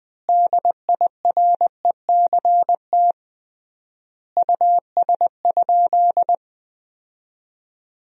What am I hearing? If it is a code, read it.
Morse code, heard as a keyed tone, about 20 wpm, 700 Hz. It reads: DIRECT US?
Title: 3x Word-Spacing - Sentences from Top 400 Words - 20wpm